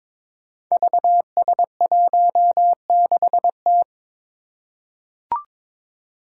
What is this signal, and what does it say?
Morse code, heard as a keyed tone, about 22 wpm, 700 Hz. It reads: VS16T
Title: Five Challenging Letter-Number Combinations 22wpm